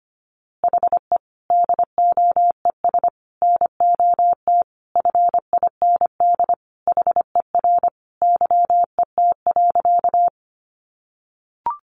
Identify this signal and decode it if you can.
Morse code, heard as a keyed tone, about 25 wpm, 700 Hz. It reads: HE DOES NOT FIND HER YET.
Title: Easy Phrases 25wpm